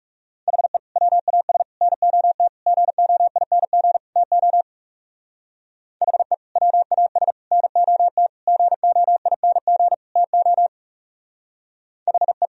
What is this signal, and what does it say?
Morse code, heard as a keyed tone, about 37 wpm, 700 Hz. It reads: HE WAS NOT GOING TO HE WAS NOT GOING TO HE
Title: Speed-Racing - Sets of 5 Words 30wpm